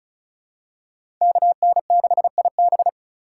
Morse code, heard as a keyed tone, about 36 wpm, 700 Hz. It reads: KN6IB
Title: USA Call Signs - 5 Characters - 2x Character Spacing - 35wpm